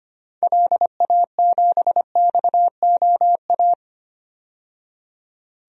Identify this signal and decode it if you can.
Morse code, heard as a keyed tone, about 25 wpm, 700 Hz. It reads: LA7XOA
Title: DXCC-100 Call Signs - 6 Characters - 25wpm